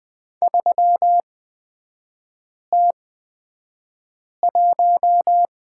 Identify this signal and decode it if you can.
Morse code, heard as a keyed tone, about 20 wpm, spaced as for 5 wpm, 700 Hz. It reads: 3T1
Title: 224 Lesson 36e - BT - Three Characters 20wpm